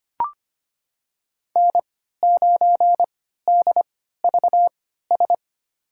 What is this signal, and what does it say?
Morse code, heard as a keyed tone, about 25 wpm, spaced as for 14 wpm, 700 Hz. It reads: N9DVS
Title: USA Call Signs - 5 Characters - 3x Character Spacing - 25wpm